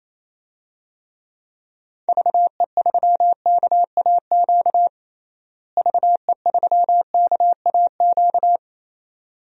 Morse code, heard as a keyed tone, about 28 wpm, 700 Hz. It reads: VE3KAQ VE3KAQ
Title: DXCC-100 Call Signs - 6 Characters - Repeated 2x - 28wpm